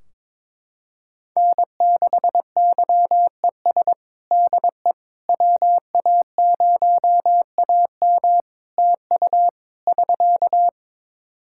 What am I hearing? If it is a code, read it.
Morse code, heard as a keyed tone, about 21 wpm, 700 Hz. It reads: N6YES DE WA0AM TU <SK>
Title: Practice QSO - Exchange 4 - Ending - 22wpm